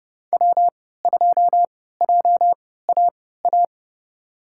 Morse code, heard as a keyed tone, about 30 wpm, spaced as for 17 wpm, 700 Hz. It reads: W2JAA
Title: USA Call Signs - 5 Characters - 3x Character Spacing - 30wpm